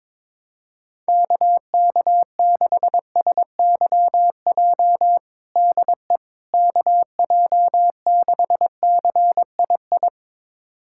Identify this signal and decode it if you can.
Morse code, heard as a keyed tone, about 22 wpm, 700 Hz. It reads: KK6SYJ DE KJ6CII
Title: USA Callsign DE Callsign - 6 Characters - 22wpm